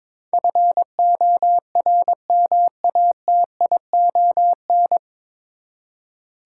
Morse code, heard as a keyed tone, about 21 wpm, 700 Hz. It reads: FORMATION
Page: Morse Code Ninja practice 22wpm